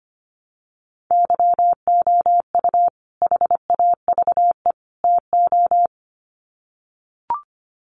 Morse code, heard as a keyed tone, about 25 wpm, 700 Hz. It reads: YOU HAVE TO
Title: Sets of 3 Words 25wpm